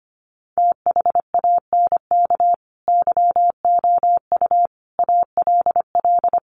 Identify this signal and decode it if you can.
Morse code, heard as a keyed tone, about 25 wpm, 700 Hz. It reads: THANK YOU ALL
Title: Sets of 5 Words Encore 25wpm